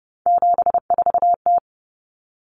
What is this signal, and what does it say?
Morse code, heard as a keyed tone, about 30 wpm, 700 Hz. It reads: 74T
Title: Common Cut Numbers - 3-Digit 30wpm